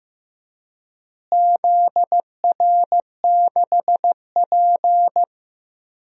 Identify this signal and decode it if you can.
Morse code, heard as a keyed tone, about 15 wpm, 700 Hz. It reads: ZR6P
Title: DXCC-100 Call Signs - 4 Characters - Repeated 3x - 15wpm